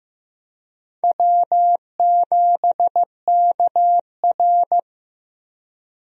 Morse code, heard as a keyed tone, about 15 wpm, 700 Hz. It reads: W7KR
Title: USA Call Signs - 4 Characters - Repeated 2x - 15wpm